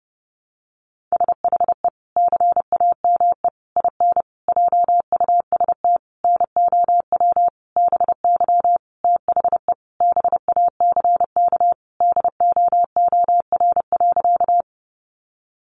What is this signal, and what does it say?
Morse code, heard as a keyed tone, about 30 wpm, 700 Hz. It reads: SHE CAME IN JUST NOW BY THE BACK DOOR.
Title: Sentences from Top 300 Words 30wpm